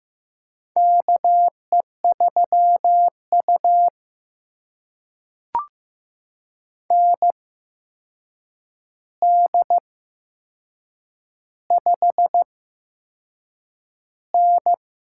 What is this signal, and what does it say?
Morse code, heard as a keyed tone, about 15 wpm, spaced as for 6 wpm, 700 Hz. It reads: KE3U ND5N
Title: USA Call Signs - 4 Characters - 8x Character Spacing - 15wpm